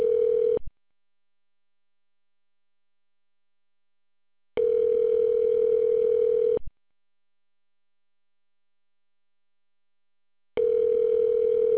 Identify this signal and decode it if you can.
Ringback tone (North America), 440 + 480 Hz.